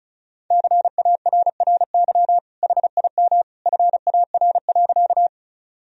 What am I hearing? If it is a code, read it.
Morse code, heard as a keyed tone, about 34 wpm, 700 Hz. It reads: CARRY HIM FAR.
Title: Sentences from Top 1000 Words - Review Part 1 - 35wpm